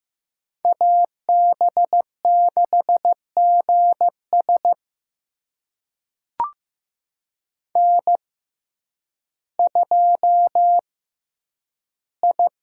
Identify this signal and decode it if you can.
Morse code, heard as a keyed tone, about 15 wpm, spaced as for 9 wpm, 700 Hz. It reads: AB6GS N2I
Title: USA Call Signs - 5 Characters - 6x Character Spacing - 15wpm